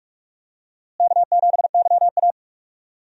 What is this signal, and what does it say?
Morse code, heard as a keyed tone, about 46 wpm, 700 Hz. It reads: K7YA
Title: USA Call Signs - 4 Characters - Repeated 2x - 45wpm